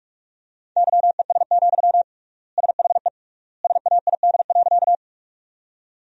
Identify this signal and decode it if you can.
Morse code, heard as a keyed tone, about 46 wpm, 700 Hz. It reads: YES, SHE SAID.
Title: 3x Word-Spacing - Sentences from Top 500 Words - 45wpm